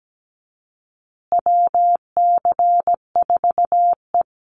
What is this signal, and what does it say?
Morse code, heard as a keyed tone, about 17 wpm, 700 Hz. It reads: WC4E
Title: Call Signs Repeated 2x - 17wpm